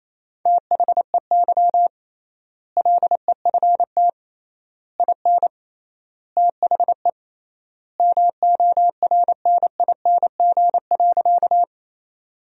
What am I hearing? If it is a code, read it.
Morse code, heard as a keyed tone, about 28 wpm, 700 Hz. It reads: THEY LEFT IN THE MORNING.
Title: 3x Word-Spacing - Sentences from Top 400 Words - 28wpm